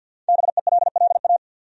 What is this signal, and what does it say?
Morse code, heard as a keyed tone, about 48 wpm, 700 Hz. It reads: BELLA
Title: USA Common Names - Max 5 Letters - 50wpm